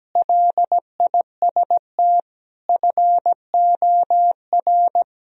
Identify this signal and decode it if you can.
Morse code, heard as a keyed tone, about 17 wpm, 700 Hz. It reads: LIST FOR
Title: Sentences from Top 1000 Words - Review Part 2 - 17wpm